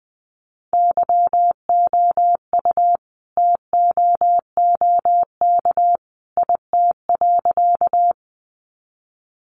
Morse code, heard as a keyed tone, about 20 wpm, 700 Hz. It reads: YOU TOOK IT.
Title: Sentences from Top 300 Words 20wpm